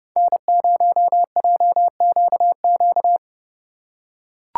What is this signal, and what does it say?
Morse code, heard as a keyed tone, about 27 wpm, 700 Hz. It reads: N0JQQ